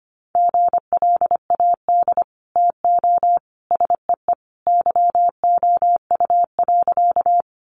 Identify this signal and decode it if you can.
Morse code, heard as a keyed tone, about 25 wpm, 700 Hz. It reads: GLAD TO SEE YOU.
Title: Sentences from Top 800 Words 25wpm